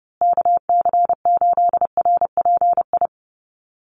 Morse code, heard as a keyed tone, about 30 wpm, 700 Hz. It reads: KC8RPI
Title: USA Call Signs Repeated 2x 30wpm